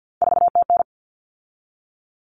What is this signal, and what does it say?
Morse code, heard as a keyed tone, about 50 wpm, 700 Hz. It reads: 4TN
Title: Common Cut Numbers - 3-Digit 50wpm